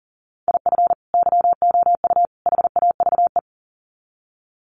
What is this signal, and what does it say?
Morse code, heard as a keyed tone, about 37 wpm, 700 Hz. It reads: IF YOU HAVE
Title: Sets of 3 Words 40wpm